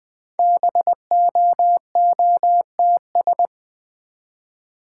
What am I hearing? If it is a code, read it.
Morse code, heard as a keyed tone, about 20 wpm, 700 Hz. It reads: BOOTS
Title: Easy Words with Hints - 20wpm